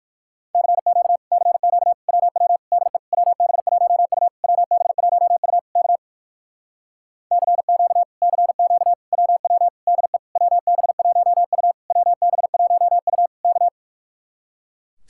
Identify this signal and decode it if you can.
Morse code, heard as a keyed tone, about 59 wpm, 700 Hz. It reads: CQ CQ WW DE WB1U WB1U K CQ CQ WW DE WB1U WB1U K